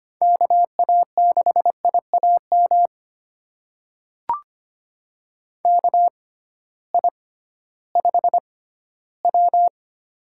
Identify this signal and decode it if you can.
Morse code, heard as a keyed tone, about 25 wpm, spaced as for 15 wpm, 700 Hz. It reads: KA6IAM KI5W
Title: USA Call Signs - 6x Character Spacing - 25wpm